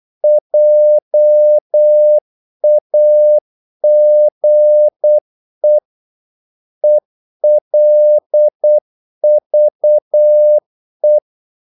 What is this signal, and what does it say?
Morse code, heard as a keyed tone, about 8 wpm, 600 Hz. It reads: JAGE ELVE
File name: Korte ord DK 8wpm